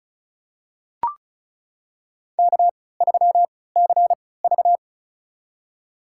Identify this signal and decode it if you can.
Morse code, heard as a keyed tone, about 36 wpm, spaced as for 20 wpm, 700 Hz. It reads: K3CV
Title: USA Call Signs - 4 Characters - 3x Character Spacing - 35wpm